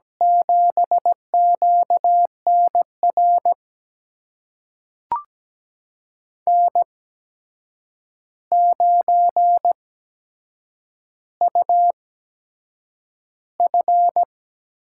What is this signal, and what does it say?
Morse code, heard as a keyed tone, about 17 wpm, spaced as for 7 wpm, 700 Hz. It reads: 7QNR N9UF
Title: USA Call Signs - 8x Character Spacing - 17wpm